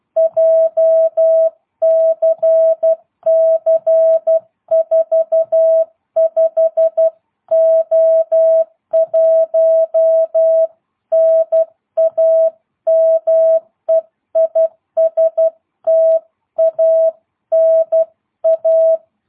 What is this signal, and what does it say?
Morse code, heard as a keyed tone, about 12 wpm, 660 Hz. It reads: JCC45O1 NAME IS TANA
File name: jh8cbh / 2022cw